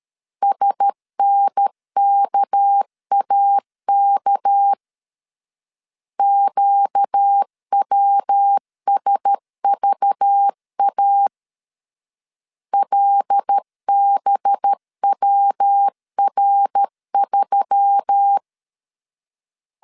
Morse code, heard as a keyed tone, about 13 wpm, 800 Hz. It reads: SNKAK QWSVA LBWR3
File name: LEZIONI / 6 PKX0 / 6 PK 12wpm 2006